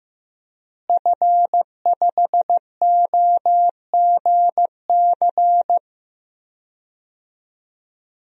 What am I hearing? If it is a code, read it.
Morse code, heard as a keyed tone, about 15 wpm, 700 Hz. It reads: F5OGC
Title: DXCC-100 Call Signs - 5 Characters - Repeated 2x - 15wpm